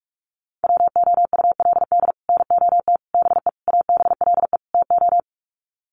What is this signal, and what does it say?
Morse code, heard as a keyed tone, about 44 wpm, 700 Hz. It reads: WOULD NOT BE ABLE TO
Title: Sets of 5 Words 45wpm